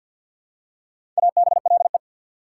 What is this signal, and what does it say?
Morse code, heard as a keyed tone, about 48 wpm, 690 Hz. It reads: ABLE
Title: Prefixes and Suffixes 50wpm